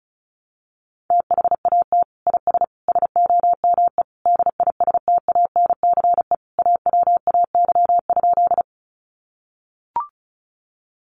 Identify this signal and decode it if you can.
Morse code, heard as a keyed tone, about 34 wpm, 700 Hz. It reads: THAT IS SOME DISTANCE AWAY?
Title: Sentences from Top 1200 Words 35wpm